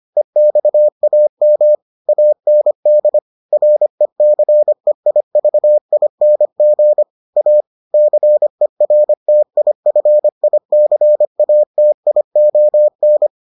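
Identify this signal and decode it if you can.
Morse code, heard as a keyed tone, about 25 wpm, 600 Hz. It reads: EXAM AND RECEIVING A CERTIFICATION
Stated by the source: Morse edition of the Foundations of Amateur Radio episodes.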